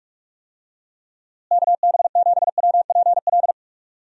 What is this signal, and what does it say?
Morse code, heard as a keyed tone, about 41 wpm, 700 Hz. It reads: KB7WPL
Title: USA Call Signs - 6 Characters - Repeated 2x - 45wpm